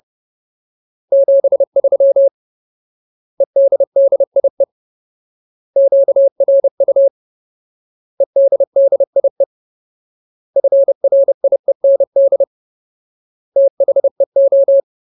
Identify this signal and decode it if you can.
Morse code, heard as a keyed tone, about 30 wpm, 550 Hz. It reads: 73 EDDIE QRU EDDIE FRIEND THEO
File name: Ultimate Word List / 30WPM Ultimate